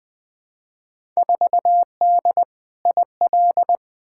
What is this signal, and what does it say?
Morse code, heard as a keyed tone, about 20 wpm, 700 Hz. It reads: 4D IL
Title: Field Day Exchanges Encore - 20wpm